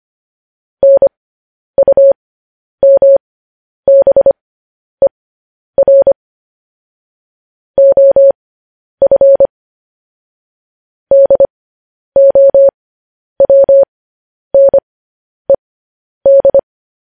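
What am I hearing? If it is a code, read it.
Morse code, heard as a keyed tone, about 25 wpm, spaced as for 10 wpm, 550 Hz. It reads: NUMBER OF DOWNED
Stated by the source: Morse code news headlines sent as a reading transcription, preceded by vvv.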